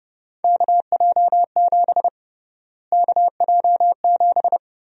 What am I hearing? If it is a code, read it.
Morse code, heard as a keyed tone, about 31 wpm, 700 Hz. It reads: KJ7 KJ7
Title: USA Call Sign Prefixes - Repeated 2x - 30wpm